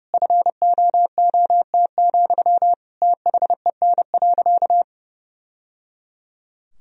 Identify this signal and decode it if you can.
Morse code, heard as a keyed tone, about 30 wpm, 700 Hz. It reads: FOOT, THEN.
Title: Sentences from Top 1000 Words - Review Part 3 - 30wpm